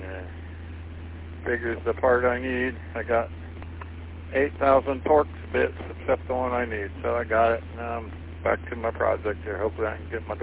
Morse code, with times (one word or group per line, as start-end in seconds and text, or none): none